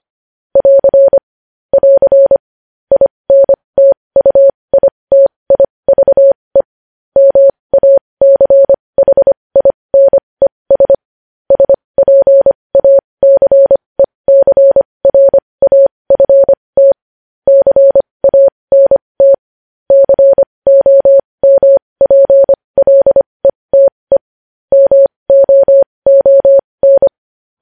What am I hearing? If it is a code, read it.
Morse code, heard as a keyed tone, about 25 wpm, spaced as for 20 wpm, 550 Hz. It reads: <AR> <AR> INTUITIVE MACHINES SPACECRAFT CANT COMPLETE MOON
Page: news-headlines-in-morse-code-at-20-wpm